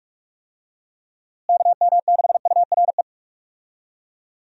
Morse code, heard as a keyed tone, about 46 wpm, 700 Hz. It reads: KM6URE
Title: USA Call Signs - 6 Characters - Repeated 3x - 45wpm